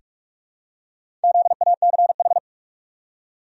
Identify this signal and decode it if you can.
Morse code, heard as a keyed tone, about 45 wpm, 700 Hz. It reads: ZACH
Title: USA Common Names - 45wpm